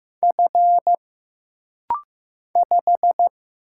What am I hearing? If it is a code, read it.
Morse code, heard as a keyed tone, about 15 wpm, 700 Hz. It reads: F 5
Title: Single Letter-Number Alt - Rapid-Fire 15wpm